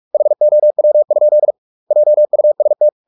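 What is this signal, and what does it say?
Morse code, heard as a keyed tone, about 44 wpm, 600 Hz. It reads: HOW? JUST
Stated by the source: Morse practice at 45 wpm, the novel Never by Ken Follett in English, from CW med Gnister.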